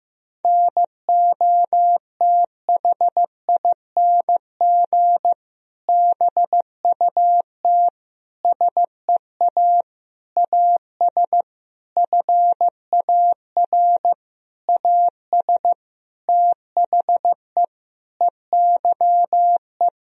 Morse code, heard as a keyed tone, about 15 wpm, 700 Hz. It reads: NOTHING BUT SEA AS FAR AS THE EYE
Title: Sentences from Top 1000 Words - Review Part 1 - 15wpm